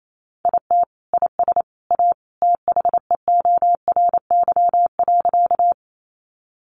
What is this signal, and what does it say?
Morse code, heard as a keyed tone, about 28 wpm, 700 Hz. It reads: IT IS A THEORY.
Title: Sentences from Top 600 Words 28wpm